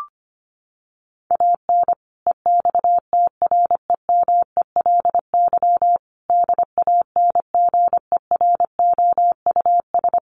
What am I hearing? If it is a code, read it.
Morse code, heard as a keyed tone, about 25 wpm, 700 Hz. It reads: AN EXTREMELY DANGEROUS
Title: Sentences from Top 1200 Words 25wpm